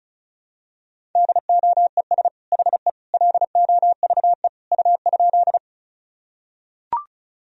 Morse code, heard as a keyed tone, about 36 wpm, 700 Hz. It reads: DOES HE LOVE U?
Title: Sentences from Top 400 Words and Abbreviations 35wpm